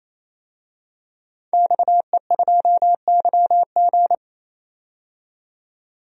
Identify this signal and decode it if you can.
Morse code, heard as a keyed tone, about 28 wpm, 700 Hz. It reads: XE2YG
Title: DXCC-100 Call Signs - 5 Characters - 28wpm